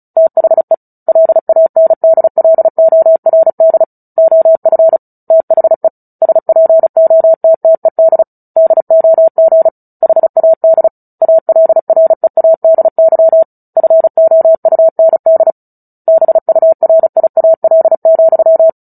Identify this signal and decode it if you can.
Morse code, heard as a keyed tone, about 34 wpm, 650 Hz. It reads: THE LANDLORD OF THE SPOTTED DOG HAD ALREADY FOUND BURIAL,
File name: Book Two / 09-Chapter 9 - 35 WPM